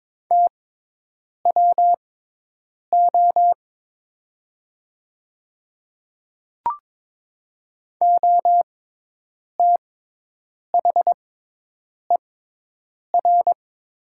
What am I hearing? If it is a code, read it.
Morse code, heard as a keyed tone, about 22 wpm, spaced as for 8 wpm, 700 Hz. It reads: TWO OTHER
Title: 6x Character Spacing - Top 100 Words - 22wpm